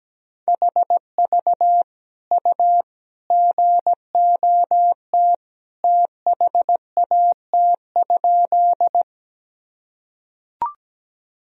Morse code, heard as a keyed tone, about 17 wpm, 700 Hz. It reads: HV U GOT THAT?
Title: Sentences from Top 300 Words and Abbreviations 17wpm